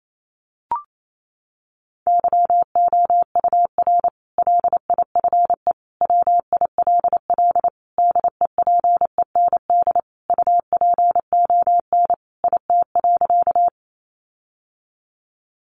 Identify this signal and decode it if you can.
Morse code, heard as a keyed tone, about 28 wpm, 700 Hz. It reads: YOUR LIFE WILL DEPEND UPON IT.
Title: Sentences from Top 1300 Words 28wpm